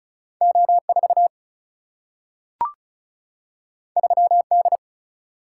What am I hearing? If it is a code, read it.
Morse code, heard as a keyed tone, about 34 wpm, 700 Hz. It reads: O4 3D
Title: Single-Letter Single-Number 35wpm